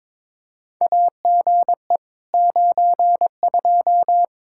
Morse code, heard as a keyed tone, about 21 wpm, 700 Hz. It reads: AGE 92
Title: Practice QSO - Exchange 3 - More Info - 22wpm